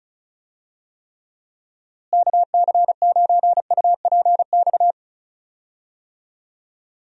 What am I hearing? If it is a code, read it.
Morse code, heard as a keyed tone, about 36 wpm, 700 Hz. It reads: KC9UPX
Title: USA Call Signs - 6 Characters - Repeated 2x - 35wpm